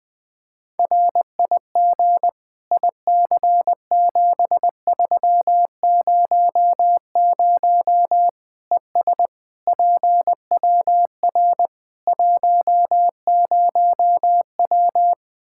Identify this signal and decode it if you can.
Morse code, heard as a keyed tone, about 19 wpm, 700 Hz. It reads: RIG IC7300 ES PWR 10W
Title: K1USN SST QSO Practice 20wpm